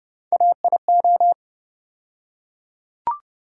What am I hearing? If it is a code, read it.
Morse code, heard as a keyed tone, about 30 wpm, 700 Hz. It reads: AIO
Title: USA Call Sign Postfixes - 3 Letters 30wpm